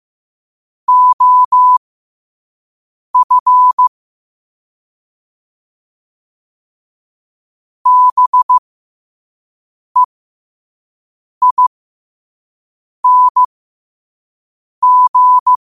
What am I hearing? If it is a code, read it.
Morse code, heard as a keyed tone, about 15 wpm, spaced as for 5 wpm, 1000 Hz. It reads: OF BEING
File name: qotdfiles / 5wpm